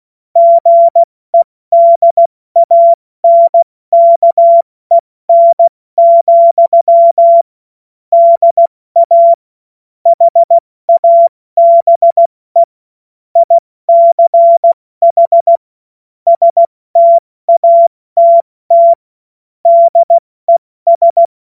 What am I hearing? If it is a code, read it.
Morse code, heard as a keyed tone, about 16 wpm, spaced as for 14 wpm, 680 Hz. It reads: GEDANKEN, DA HABE ICH STATT DES
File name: Witz-09-September-2025-70-F80